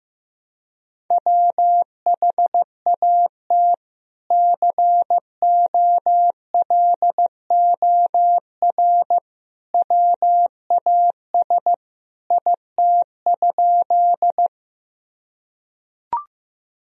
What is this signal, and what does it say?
Morse code, heard as a keyed tone, about 15 wpm, 700 Hz. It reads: WHAT COLOR WAS IT?
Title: Sentences from Top 1000 Words - Review Part 2 - 15wpm